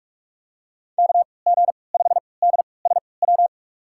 Morse code, heard as a keyed tone, about 46 wpm, spaced as for 25 wpm, 700 Hz. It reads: KG5DSW
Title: USA Call Signs - 6 Characters - 3x Character Spacing - 45wpm